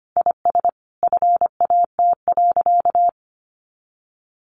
Morse code, heard as a keyed tone, about 25 wpm, 700 Hz. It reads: IS FAT.
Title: Easy Phrases 25wpm